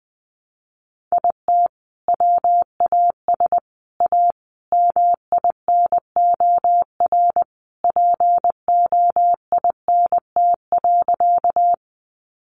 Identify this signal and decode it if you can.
Morse code, heard as a keyed tone, about 20 wpm, 700 Hz. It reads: IT WAS A MINOR POINT.
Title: Sentences from Top 1500 Words 20wpm